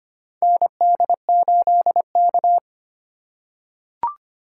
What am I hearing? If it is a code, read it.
Morse code, heard as a keyed tone, about 25 wpm, 700 Hz. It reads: ND8K
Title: USA Call Signs - 4 Characters - 3x Character Spacing - 25wpm